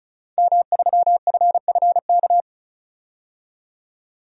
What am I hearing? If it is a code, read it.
Morse code, heard as a keyed tone, about 34 wpm, 700 Hz. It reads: M3FFK
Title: DXCC-100 Call Signs - 5 Characters - 35wpm